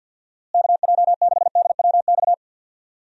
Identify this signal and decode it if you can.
Morse code, heard as a keyed tone, about 47 wpm, 700 Hz. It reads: KJ6DWX
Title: USA Call Signs - 6 Characters - Repeated 3x - 50wpm